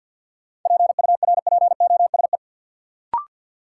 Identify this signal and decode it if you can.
Morse code, heard as a keyed tone, about 53 wpm, 700 Hz. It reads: PURPOSE
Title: Sets of 5 Words - New Word Drill - 50wpm